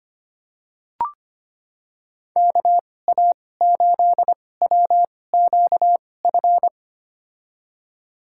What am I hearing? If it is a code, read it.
Morse code, heard as a keyed tone, about 25 wpm, spaced as for 18 wpm, 700 Hz. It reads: KA8WQF
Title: USA Call Signs - 6 Characters - 2x Character Spacing - 25wpm